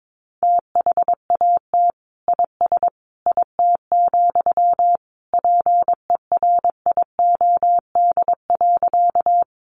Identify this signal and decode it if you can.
Morse code, heard as a keyed tone, about 22 wpm, 700 Hz. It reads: THAT IS IT, PERIOD.
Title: Sentences from Top 800 Words 22wpm